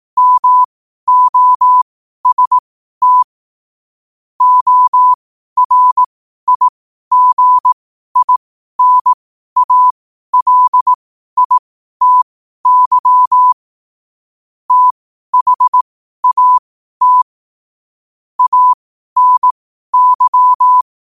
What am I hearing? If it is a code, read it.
Morse code, heard as a keyed tone, about 18 wpm, spaced as for 12 wpm, 1000 Hz. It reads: MOST ORIGINALITY THAT ANY
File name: qotdfiles / 12wpm